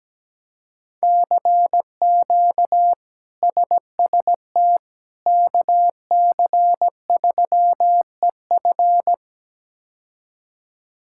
Morse code, heard as a keyed tone, about 17 wpm, 700 Hz. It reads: CQ SST KC3EF
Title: K1USN SST QSO Practice - 17wpm